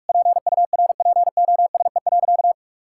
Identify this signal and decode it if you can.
Morse code, heard as a keyed tone, about 46 wpm, 700 Hz. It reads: PURPOSE.
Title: Sentences from Top 1600 Words 45wpm